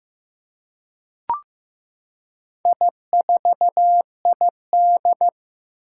Morse code, heard as a keyed tone, about 15 wpm, 700 Hz. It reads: I4ID
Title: Four Challenging Letter-Number Combinations 15wpm